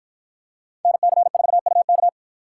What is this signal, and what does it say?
Morse code, heard as a keyed tone, about 50 wpm, 700 Hz. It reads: NC4UK